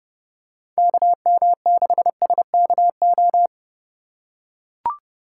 Morse code, heard as a keyed tone, about 30 wpm, 700 Hz. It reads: KM6SKO
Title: USA Call Signs - 6 Characters - 3x Character Spacing - 30wpm